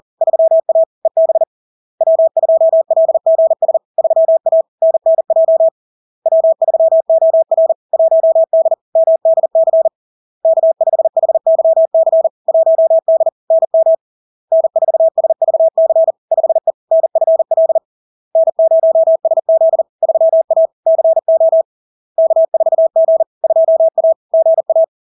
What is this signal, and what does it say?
Morse code, heard as a keyed tone, about 40 wpm, 650 Hz. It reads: 3A EB W2LGS 3A NNJ W3OR 1D MDC K5HYC 1D NM N4SVC 5E NFL N0SZ 3A CO K4G 2A GA
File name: LARK-FD-2022-Log-in-CW-40wpm-650-Hz